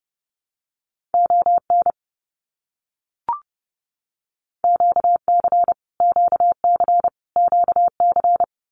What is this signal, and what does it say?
Morse code, heard as a keyed tone, about 30 wpm, 700 Hz. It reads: ON QC QC QC
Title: Canadian Provinces and Territories - Repeated 3x - 30wpm